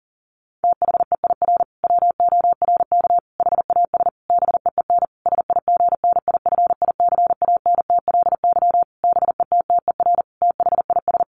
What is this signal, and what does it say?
Morse code, heard as a keyed tone, about 40 wpm, 700 Hz. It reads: THEIR WORK HAS BEEN SIGNIFICANTLY BETTER THIS
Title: Sentences from Top 1500 Words 40wpm